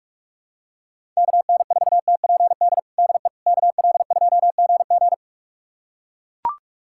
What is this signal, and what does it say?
Morse code, heard as a keyed tone, about 42 wpm, 690 Hz. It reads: KN4TPD DE KL2GG
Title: USA Callsign DE Callsign 45wpm